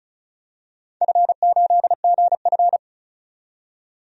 Morse code, heard as a keyed tone, about 36 wpm, 690 Hz. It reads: F8GF
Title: DXCC-100 Call Signs - 4 Characters - 35wpm